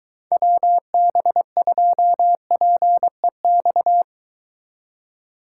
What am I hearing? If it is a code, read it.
Morse code, heard as a keyed tone, about 22 wpm, 700 Hz. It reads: WB2PEX